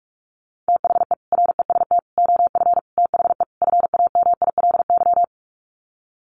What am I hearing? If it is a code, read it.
Morse code, heard as a keyed tone, about 44 wpm, 700 Hz. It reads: THE REST OF THE FAMILY
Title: Sets of 5 Words Encore2 45wpm